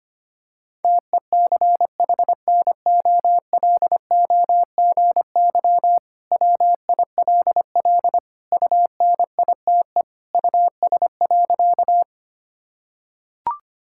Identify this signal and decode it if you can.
Morse code, heard as a keyed tone, about 25 wpm, 700 Hz. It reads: TECHNOLOGY WILL UNITE US.
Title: Sentences from Top 1000 Words - Review Part 5 - 25wpm